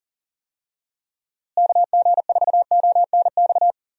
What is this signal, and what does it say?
Morse code, heard as a keyed tone, about 40 wpm, 700 Hz. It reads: KG4ONX
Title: USA Call Signs - 6 Characters - Repeated 2x - 40wpm